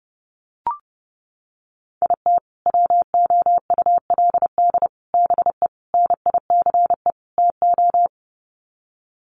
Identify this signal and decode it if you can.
Morse code, heard as a keyed tone, about 30 wpm, 700 Hz. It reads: IT WOULD BE NICE TO
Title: Sets of 5 Words Encore2 30wpm